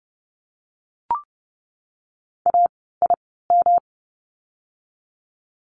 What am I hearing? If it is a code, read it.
Morse code, heard as a keyed tone, about 30 wpm, spaced as for 17 wpm, 700 Hz. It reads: AIM
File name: course / 30:10wpm / 108 Lesson 18f - Letter m - Words 30wpm